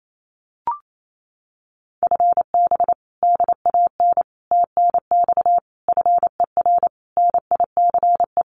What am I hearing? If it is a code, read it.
Morse code, heard as a keyed tone, about 28 wpm, 700 Hz. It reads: FB DAN TNX FER NICE
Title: Common QSO Phrases Encore 28wpm